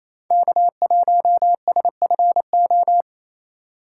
Morse code, heard as a keyed tone, about 33 wpm, 700 Hz. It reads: K1SFO
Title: Speed Racing - USA Call Signs 28wpm